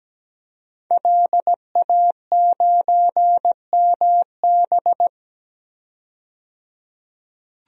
Morse code, heard as a keyed tone, about 17 wpm, 700 Hz. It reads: LA9MB
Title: DXCC-100 Call Signs - 5 Characters - 17wpm